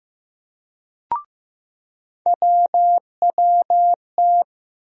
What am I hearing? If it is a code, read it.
Morse code, heard as a keyed tone, about 15 wpm, 700 Hz. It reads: WWT
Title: USA Call Sign Postfixes - 3 Letters 15wpm